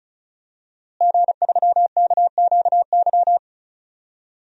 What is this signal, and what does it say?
Morse code, heard as a keyed tone, about 34 wpm, 700 Hz. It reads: G3KQY
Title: DXCC-100 Call Signs - 5 Characters - Repeated 3x - 35wpm